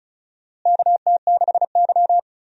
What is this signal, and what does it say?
Morse code, heard as a keyed tone, about 36 wpm, 700 Hz. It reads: KT6Y
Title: USA Call Signs - 4 Characters - Repeated 3x - 35wpm